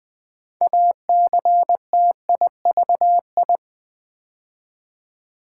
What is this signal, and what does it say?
Morse code, heard as a keyed tone, about 20 wpm, 700 Hz. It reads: ACTIVI
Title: Build A Word - 3 Letters At A Time - 8-Letter Words - 20wpm